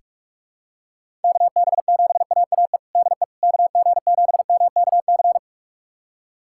Morse code, heard as a keyed tone, about 46 wpm, 700 Hz. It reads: KB7AAE DE KG7MKC
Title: USA Callsign DE Callsign - 6 Characters - 45wpm